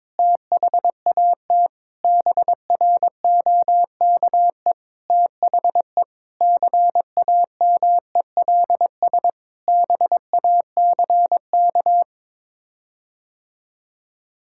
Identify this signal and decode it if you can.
Morse code, heard as a keyed tone, about 22 wpm, 700 Hz. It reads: THAT BROKE THE CAMELS BACK
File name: practice / 22wpm / Sets of 7 Words - 22wpm